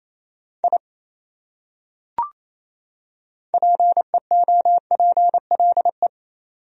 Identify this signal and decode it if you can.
Morse code, heard as a keyed tone, about 28 wpm, 700 Hz. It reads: I PEOPLE
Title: Top 100 Words 28wpm